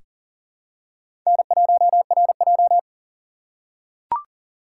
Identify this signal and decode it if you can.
Morse code, heard as a keyed tone, about 40 wpm, 700 Hz. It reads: N1RJ
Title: USA Call Signs - 4 Characters - 3x Character Spacing - 40wpm